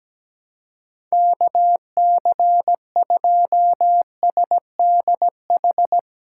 Morse code, heard as a keyed tone, about 17 wpm, 700 Hz. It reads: KC2SDH
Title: USA Call Signs - 2x Character Spacing - 17wpm